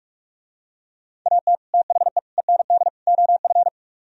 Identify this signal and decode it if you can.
Morse code, heard as a keyed tone, about 46 wpm, 700 Hz. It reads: AT THE END OF